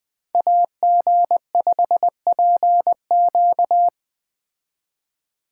Morse code, heard as a keyed tone, about 20 wpm, 700 Hz. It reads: AG5PQ